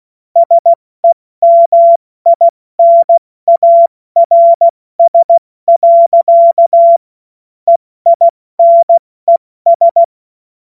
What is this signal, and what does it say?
Morse code, heard as a keyed tone, about 16 wpm, 680 Hz. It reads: SEMINARS. EINES
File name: Morsekurs / Witze / Witz 14. Juni 2022 70 F80